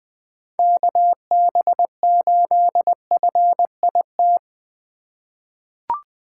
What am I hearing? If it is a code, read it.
Morse code, heard as a keyed tone, about 20 wpm, 700 Hz. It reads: KB8FIT
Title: USA Call Signs - 6 Characters - 4x Character Spacing - 20wpm